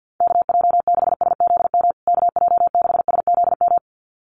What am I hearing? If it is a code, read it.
Morse code, heard as a keyed tone, about 48 wpm, 700 Hz. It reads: KJ6SZM KJ6SZM
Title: USA Call Signs Repeated 2x 50wpm